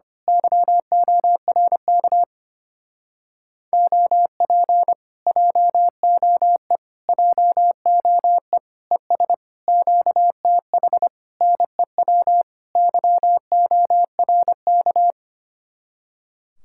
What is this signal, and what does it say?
Morse code, heard as a keyed tone, about 24 wpm, 700 Hz. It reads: YORK OP JOE JOE ES QTH NEW YORK
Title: Speed-Racing Common QSO Phrases - 25wpm